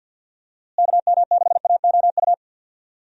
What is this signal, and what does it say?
Morse code, heard as a keyed tone, about 53 wpm, 700 Hz. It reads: KM6AOU
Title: USA Call Signs - 6 Characters - Repeated 3x - 50wpm